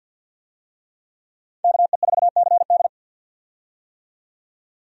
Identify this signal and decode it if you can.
Morse code, heard as a keyed tone, about 51 wpm, 700 Hz. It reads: KE4CD
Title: USA Call Signs - 5 Characters - 50wpm